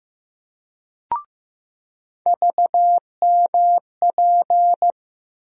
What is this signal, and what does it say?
Morse code, heard as a keyed tone, about 15 wpm, 700 Hz. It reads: VMP
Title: USA Call Sign Postfixes - 3 Letters 15wpm